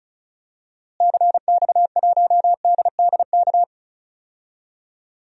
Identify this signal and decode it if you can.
Morse code, heard as a keyed tone, about 34 wpm, 700 Hz. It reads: CX1DDK
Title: DXCC-100 Call Signs - 6 Characters - Repeated 3x - 35wpm